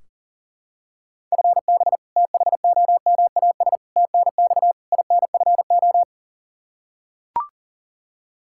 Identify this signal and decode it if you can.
Morse code, heard as a keyed tone, about 40 wpm, 700 Hz. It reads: FB THOMAS TNX INFO
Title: Practice QSO - Exchange 3 - More Info - 40wpm